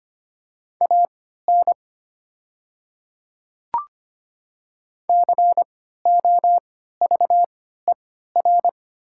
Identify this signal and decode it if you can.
Morse code, heard as a keyed tone, about 25 wpm, spaced as for 14 wpm, 700 Hz. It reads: AN COVER
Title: 3x Character Spacing - Top 300 Words - 25wpm